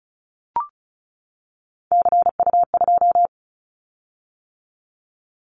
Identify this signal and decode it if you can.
Morse code, heard as a keyed tone, about 34 wpm, 700 Hz. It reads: CU2
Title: Callsign Prefixes for Top 100 DXCC - 35wpm